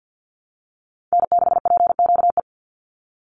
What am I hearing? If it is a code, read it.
Morse code, heard as a keyed tone, about 50 wpm, 700 Hz. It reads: N6PQE
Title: USA Call Signs 50wpm